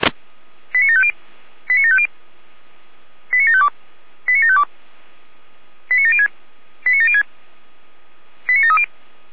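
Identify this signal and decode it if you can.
Five-tone selective call (ZVEI), heard as quick runs of five tones, tones of about 70 ms. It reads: 89750 89750 89752 89752 89706 89706 89730
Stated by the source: Five-tone ZVEI selective call 89485, received by SDR.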